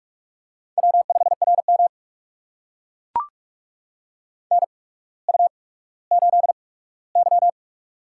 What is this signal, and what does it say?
Morse code, heard as a keyed tone, about 44 wpm, spaced as for 19 wpm, 700 Hz. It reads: W5RM NU8Y
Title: USA Call Signs - 4 Characters - 8x Character Spacing - 45wpm